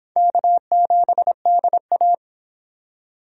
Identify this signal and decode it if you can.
Morse code, heard as a keyed tone, about 29 wpm, 700 Hz. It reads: K7DA